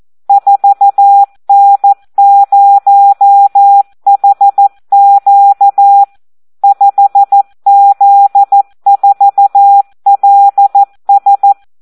Morse code, heard as a keyed tone, about 14 wpm, 800 Hz. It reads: 4N0HQ 5Z4LS
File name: Audio practica CW / SD 1 14